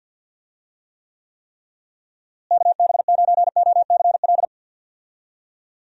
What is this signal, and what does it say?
Morse code, heard as a keyed tone, about 49 wpm, 700 Hz. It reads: KB9OCL
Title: USA Call Signs - 6 Characters - Repeated 3x - 50wpm